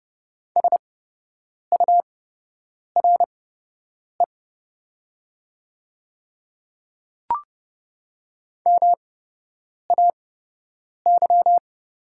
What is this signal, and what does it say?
Morse code, heard as a keyed tone, about 30 wpm, spaced as for 8 wpm, 700 Hz. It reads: SURE MAY
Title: 8x Character Spacing - Top 300 Words - 30wpm